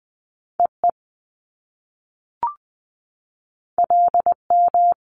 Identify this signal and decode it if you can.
Morse code, heard as a keyed tone, about 20 wpm, 700 Hz. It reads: EE LM
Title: Top Two Letter Combinations in Words 20wpm